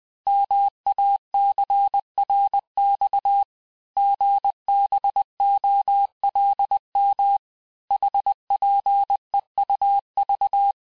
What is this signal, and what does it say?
Morse code, heard as a keyed tone, about 20 wpm, 800 Hz. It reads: MACRX GBOLM HPEUV
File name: cours-858-texte-chiffre-vitesse-19mpm